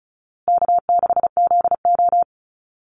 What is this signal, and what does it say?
Morse code, heard as a keyed tone, about 34 wpm, 700 Hz. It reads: K6ZO
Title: Easy Call Signs Repeated 3x - 35wpm